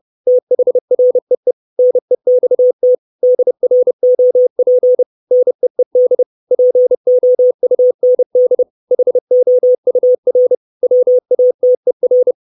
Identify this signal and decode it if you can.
Morse code, heard as a keyed tone, about 30 wpm, 500 Hz. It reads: THREE NEXT DROP NEED POUND HOUR WATER
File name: 632 Words 30WPM